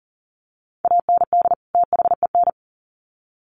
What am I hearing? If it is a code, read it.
Morse code, heard as a keyed tone, about 40 wpm, 700 Hz. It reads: AND THEN
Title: Sets of 2 Words 40wpm